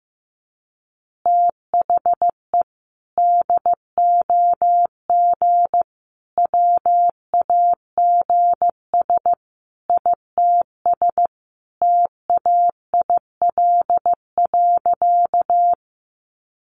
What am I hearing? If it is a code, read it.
Morse code, heard as a keyed tone, about 15 wpm, 700 Hz. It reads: THE DOG WAGS ITS TAIL.
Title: Easy Phrases 15wpm